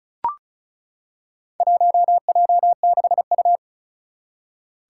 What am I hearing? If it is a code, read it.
Morse code, heard as a keyed tone, about 34 wpm, 700 Hz. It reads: 1J6U
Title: Four Challenging Letter-Number Combinations 35wpm